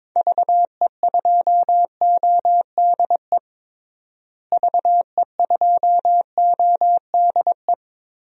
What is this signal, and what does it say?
Morse code, heard as a keyed tone, about 21 wpm, 700 Hz. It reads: VE2ODE VE2ODE
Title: DXCC-100 Call Signs - 6 Characters - Repeated 2x - 22wpm